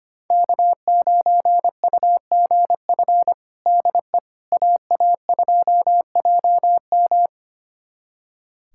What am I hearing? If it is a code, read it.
Morse code, heard as a keyed tone, about 25 wpm, 700 Hz. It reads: K9UGF DE AA2JM
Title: USA Callsign DE Callsign - 5 Characters - 25wpm